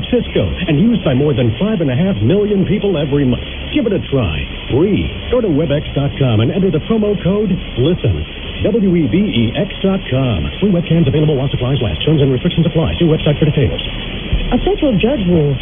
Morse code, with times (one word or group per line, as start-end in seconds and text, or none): none